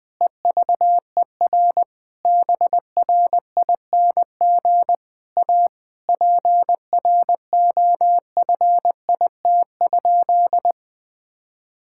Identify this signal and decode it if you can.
Morse code, heard as a keyed tone, about 20 wpm, 700 Hz. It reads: EVER BRING A PROFIT?
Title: Sentences from Top 1000 Words - Review Part 5 - 20wpm